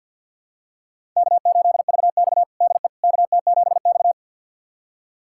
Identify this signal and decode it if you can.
Morse code, heard as a keyed tone, about 47 wpm, 700 Hz. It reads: K8VX DE KT7X
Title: USA Callsign DE Callsign - 4 Characters - 50wpm